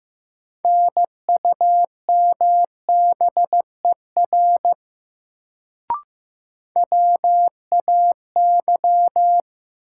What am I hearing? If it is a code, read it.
Morse code, heard as a keyed tone, about 15 wpm, 700 Hz. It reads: NUMBER WAY
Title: Sentences from Top 100 Words - New Word Drill - 15wpm